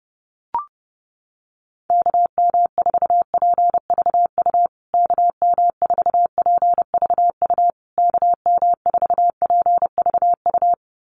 Morse code, heard as a keyed tone, about 30 wpm, 700 Hz. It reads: KM4PVU KM4PVU KM4PVU
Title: USA Call Signs Repeated 3x 30wpm